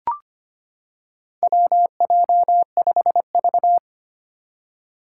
Morse code, heard as a keyed tone, about 25 wpm, 700 Hz. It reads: WJ5V
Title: Four Challenging Letter-Number Combinations 25wpm